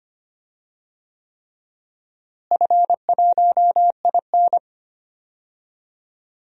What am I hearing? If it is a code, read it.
Morse code, heard as a keyed tone, about 25 wpm, 700 Hz. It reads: F1IN
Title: DXCC-100 Call Signs - 4 Characters - Repeated 3x - 25wpm